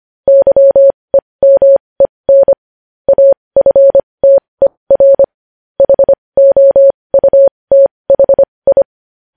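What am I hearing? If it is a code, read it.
Morse code, heard as a keyed tone, about 25 wpm, spaced as for 20 wpm, 550 Hz. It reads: YEMEN AFTER HOUTHI
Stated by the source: Morse news headlines sent at 20.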